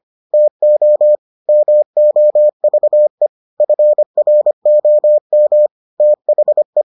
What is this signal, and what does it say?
Morse code, heard as a keyed tone, about 25 wpm, 600 Hz. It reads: TO MOVE FROM THE